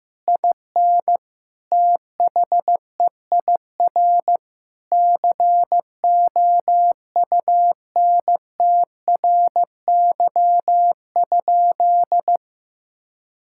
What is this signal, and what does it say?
Morse code, heard as a keyed tone, about 15 wpm, 700 Hz. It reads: IN THEIR COUNTRY?
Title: Sentences from Top 1000 Words - Review Part 5 - 15wpm